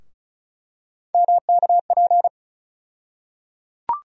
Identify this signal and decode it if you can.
Morse code, heard as a keyed tone, about 36 wpm, 700 Hz. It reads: MKP
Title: USA Call Sign Postfixes - 3 Letters 35wpm